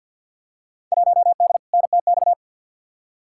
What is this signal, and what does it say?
Morse code, heard as a keyed tone, about 48 wpm, 700 Hz. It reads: 1D NTX
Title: Field Day Exchanges - 50wpm